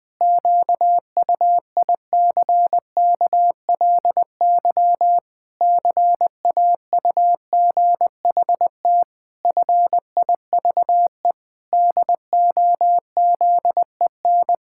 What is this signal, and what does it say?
Morse code, heard as a keyed tone, about 20 wpm, 700 Hz. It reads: QUICKLY CAUGHT FIVE DOZEN
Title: Pangram Challenge v2 - 20wpm